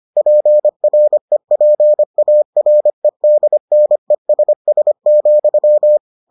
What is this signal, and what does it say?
Morse code, heard as a keyed tone, about 25 wpm, 600 Hz. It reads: PREPAREDNESS,